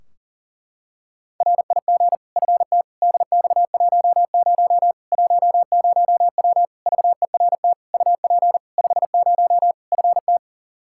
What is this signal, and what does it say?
Morse code, heard as a keyed tone, about 40 wpm, 700 Hz. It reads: RIG FT DX10 10W VERT UP 50 FT
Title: Practice QSO - Exchange 2 - Info - 40wpm